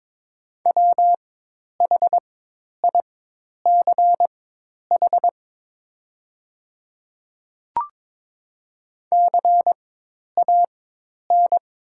Morse code, heard as a keyed tone, about 22 wpm, spaced as for 10 wpm, 700 Hz. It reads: WHICH CAN
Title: 4x Character Spacing - Top 200 Words - 22wpm